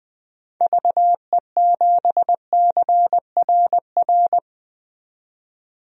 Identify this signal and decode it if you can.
Morse code, heard as a keyed tone, about 20 wpm, 700 Hz. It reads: VE7CRR
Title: DXCC-100 Call Signs - 6 Characters - 20wpm